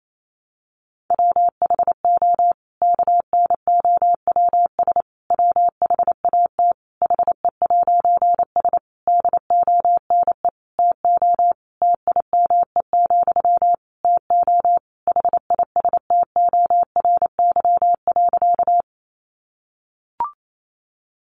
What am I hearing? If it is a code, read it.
Morse code, heard as a keyed tone, about 28 wpm, 700 Hz. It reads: WHO KNOWS WHAT HE'S DONE TO TIME, TO HISTORY.
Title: A Sound Of Thunder 28wpm